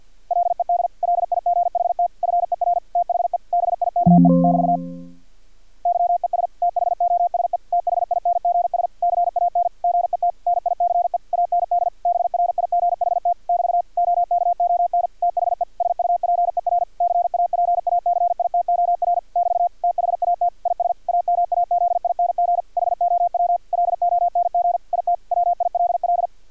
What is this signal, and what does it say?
Morse code, heard as a keyed tone, about 50 wpm, 700 Hz. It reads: LED LIGHT FER THE BIKE <BT> YES THOSE THINGS CAN GET NICE AND BRIGHT <BT> OKON THE SUPER CAPACITOR <BT> THAT IS AMAZING HOW LONG IT WILL
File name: archived / CRSnet